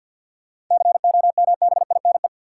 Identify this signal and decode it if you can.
Morse code, heard as a keyed tone, about 52 wpm, 700 Hz. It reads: COMBINE
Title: Sentences from Top 1600 Words - New Word Drill Alt - 50wpm